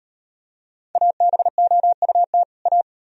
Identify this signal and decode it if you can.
Morse code, heard as a keyed tone, about 38 wpm, 700 Hz. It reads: ABOUT A